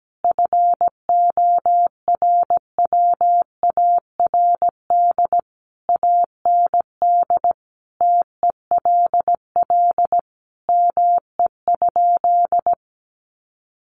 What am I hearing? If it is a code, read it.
Morse code, heard as a keyed tone, about 17 wpm, 700 Hz. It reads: FORWARD AND TELL ME?
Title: Sentences from Top 1300 Words 17wpm